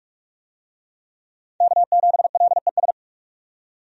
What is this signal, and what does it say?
Morse code, heard as a keyed tone, about 46 wpm, 700 Hz. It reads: K7LES
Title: USA Call Signs - 5 Characters - 45wpm